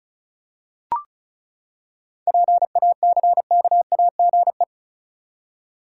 Morse code, heard as a keyed tone, about 36 wpm, 700 Hz. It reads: PACKAGE
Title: Seven Letter Words - 35wpm